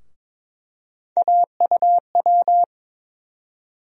Morse code, heard as a keyed tone, about 21 wpm, 700 Hz. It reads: AUW
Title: USA Call Sign Postfixes - 3 Letters 22wpm